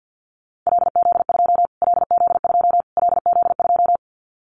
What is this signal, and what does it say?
Morse code, heard as a keyed tone, about 50 wpm, 700 Hz. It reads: LZ2 LZ2 LZ2
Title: Callsign Prefixes for Top 25 DXCC - Repeated 3x - 50wpm